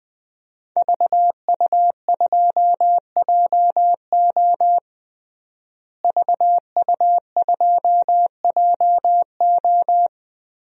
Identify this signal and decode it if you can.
Morse code, heard as a keyed tone, about 20 wpm, 700 Hz. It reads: VU2JO VU2JO
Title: DXCC-100 Call Signs - 5 Characters - Repeated 3x - 20wpm